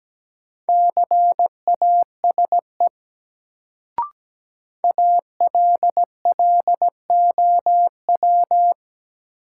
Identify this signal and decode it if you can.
Morse code, heard as a keyed tone, about 17 wpm, 700 Hz. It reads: CASE ALLOW
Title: Sentences from Top 700 Words - New Word Drill - 17wpm